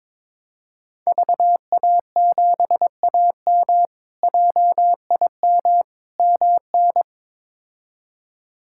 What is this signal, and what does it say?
Morse code, heard as a keyed tone, about 22 wpm, 700 Hz. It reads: VA7AM JIM MN
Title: K1USN SST QSO Practice - 22wpm